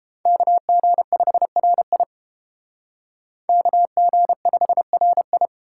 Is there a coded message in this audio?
Yes: Morse code from 0.3 to 5.5 s, heard as a keyed tone, about 31 wpm, 700 Hz. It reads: KG5RI KG5RI